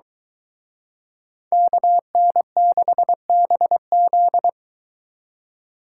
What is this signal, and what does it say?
Morse code, heard as a keyed tone, about 23 wpm, 700 Hz. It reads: KN6BZ